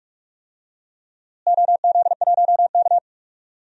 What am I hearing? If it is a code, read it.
Morse code, heard as a keyed tone, about 45 wpm, 700 Hz. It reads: OZ1K
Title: DXCC-100 Call Signs - 4 Characters - Repeated 3x - 45wpm